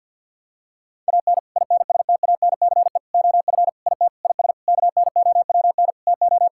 Morse code, heard as a keyed tone, about 51 wpm, 690 Hz. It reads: AN INSTANCE OF IT IS KNOWN TO
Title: Sentences from Top 1000 Words - Review Part 5 - 50wpm